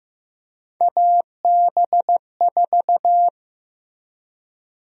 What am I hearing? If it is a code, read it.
Morse code, heard as a keyed tone, about 15 wpm, 700 Hz. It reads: AB4
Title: USA Call Sign Prefixes - Repeated 2x - 15wpm